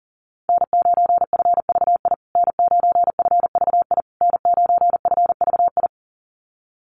Morse code, heard as a keyed tone, about 37 wpm, 700 Hz. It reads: N9FVI N9FVI N9FVI
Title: USA Call Signs Repeated 3x 40wpm